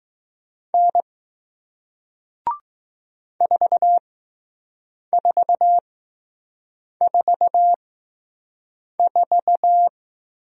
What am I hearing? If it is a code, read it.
Morse code, heard as a keyed tone, about 20 wpm, spaced as for 6 wpm, 700 Hz. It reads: N 4444